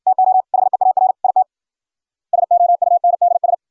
Morse code, heard as a keyed tone, about 61 wpm, 690 Hz. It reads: TO HEAR IT SOUNDS